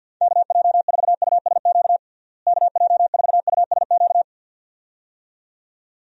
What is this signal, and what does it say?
Morse code, heard as a keyed tone, about 51 wpm, 700 Hz. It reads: KJ4USQ KJ4USQ
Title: USA Call Signs - 6 Characters - Repeated 2x - 50wpm